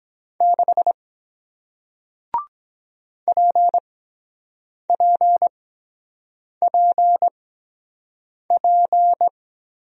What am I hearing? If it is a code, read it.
Morse code, heard as a keyed tone, about 23 wpm, spaced as for 7 wpm, 700 Hz. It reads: 6 PPPP